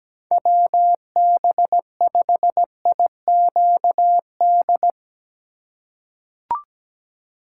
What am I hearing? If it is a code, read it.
Morse code, heard as a keyed tone, about 17 wpm, 700 Hz. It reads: WB5IQD
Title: USA Call Signs - 4x Character Spacing - 17wpm